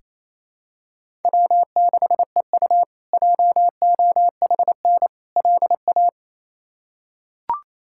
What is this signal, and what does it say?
Morse code, heard as a keyed tone, about 28 wpm, 700 Hz. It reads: W6EU JOHN LA
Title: K1USN SST QSO Practice - 28wpm